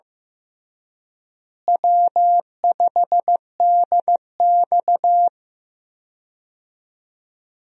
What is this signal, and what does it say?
Morse code, heard as a keyed tone, about 15 wpm, 700 Hz. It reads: W5DX
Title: USA Call Signs - 4 Characters - Repeated 2x - 15wpm